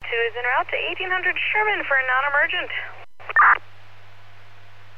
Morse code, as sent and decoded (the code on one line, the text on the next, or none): none